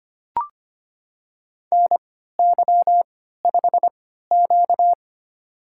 Morse code, heard as a keyed tone, about 25 wpm, spaced as for 14 wpm, 700 Hz. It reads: NY5Q
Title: USA Call Signs - 4 Characters - 3x Character Spacing - 25wpm